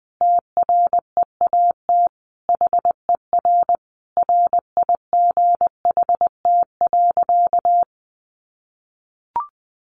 Morse code, heard as a keyed tone, about 20 wpm, 700 Hz. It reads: TREAT HER RIGHT.
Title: Sentences from Top 1300 Words 20wpm